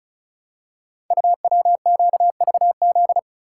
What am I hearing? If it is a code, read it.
Morse code, heard as a keyed tone, about 36 wpm, 700 Hz. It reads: UWQVZ
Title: Code Groups - 5 Letters - 35wpm